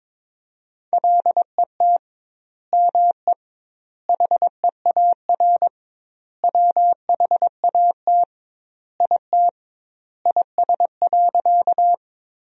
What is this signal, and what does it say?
Morse code, heard as a keyed tone, about 22 wpm, 700 Hz. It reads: LET ME HEAR WHAT IT IS.
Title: 2x Word-Spacing - Sentences from Top 300 Words - 22wpm